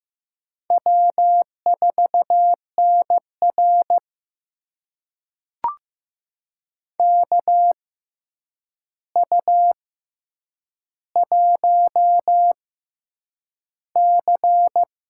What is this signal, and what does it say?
Morse code, heard as a keyed tone, about 15 wpm, spaced as for 8 wpm, 700 Hz. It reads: W4NR KU1C
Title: USA Call Signs - 4 Characters - 6x Character Spacing - 15wpm